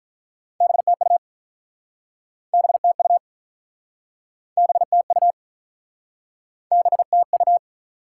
Morse code, heard as a keyed tone, about 47 wpm, 700 Hz. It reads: BTU BTU BTU BTU